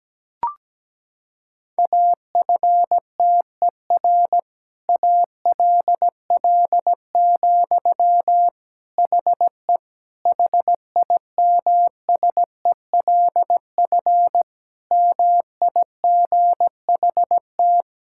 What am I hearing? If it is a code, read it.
Morse code, heard as a keyed tone, about 17 wpm, 700 Hz. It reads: AFTER ALL, HE HIMSELF MIGHT
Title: Sentences from Top 1600 Words 17wpm